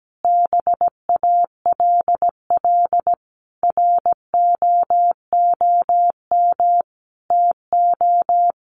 Morse code, heard as a keyed tone, about 17 wpm, 700 Hz. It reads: BALL ROOM TO
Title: Sentences from Top 1300 Words 17wpm